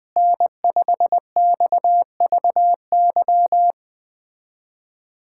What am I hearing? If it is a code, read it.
Morse code, heard as a keyed tone, about 20 wpm, 700 Hz. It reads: N5XVY
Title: USA Callsign DE Callsign 20wpm